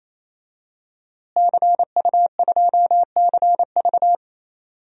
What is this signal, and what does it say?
Morse code, heard as a keyed tone, about 28 wpm, 700 Hz. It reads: CU2CV
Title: DXCC-100 Call Signs - 5 Characters - Repeated 3x - 28wpm